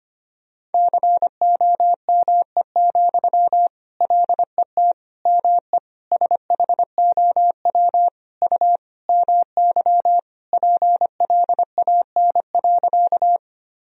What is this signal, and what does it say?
Morse code, heard as a keyed tone, about 25 wpm, 700 Hz. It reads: COME, LET ME SHOW U MY PLAN.
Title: Sentences from Top 500 Words and Abbreviations 25wpm